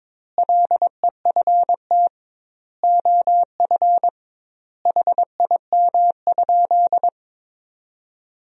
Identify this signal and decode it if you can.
Morse code, heard as a keyed tone, about 22 wpm, 700 Hz. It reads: LEFT OF HIM?
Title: 2x Word-Spacing - Sentences from Top 300 Words - 22wpm